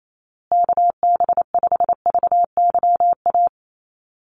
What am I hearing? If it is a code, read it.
Morse code, heard as a keyed tone, about 28 wpm, 700 Hz. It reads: KB5VYA
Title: USA Call Signs Repeated 2x 28wpm